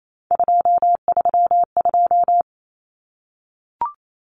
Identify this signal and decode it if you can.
Morse code, heard as a keyed tone, about 28 wpm, 700 Hz. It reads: 232
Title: Common Cut Numbers - 3-Digit 28wpm